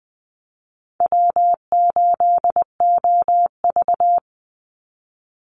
Morse code, heard as a keyed tone, about 20 wpm, 700 Hz. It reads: W8OV
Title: Easy Call Signs - 20wpm